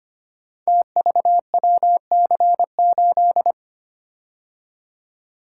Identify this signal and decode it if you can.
Morse code, heard as a keyed tone, about 25 wpm, 700 Hz. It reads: TVWC8
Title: Code Groups - 5 Letter:Numbers - 25wpm